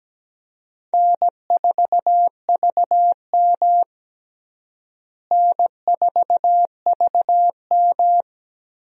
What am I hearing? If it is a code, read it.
Morse code, heard as a keyed tone, about 17 wpm, 700 Hz. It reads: N4VM N4VM
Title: USA Call Signs - 4 Characters - Repeated 3x - 17wpm